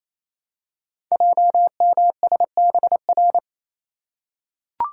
Morse code, heard as a keyed tone, about 27 wpm, 700 Hz. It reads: JMSBR
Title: Code Groups - 5 Letters - 28wpm